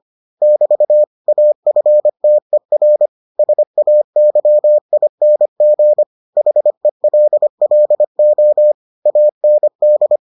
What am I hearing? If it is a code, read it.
Morse code, heard as a keyed tone, about 25 wpm, 600 Hz. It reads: <BT> AFTER SAYING HELLO AND